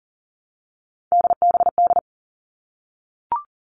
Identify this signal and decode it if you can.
Morse code, heard as a keyed tone, about 40 wpm, 700 Hz. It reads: DBD
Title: All Cut Numbers - 3-Digit 40wpm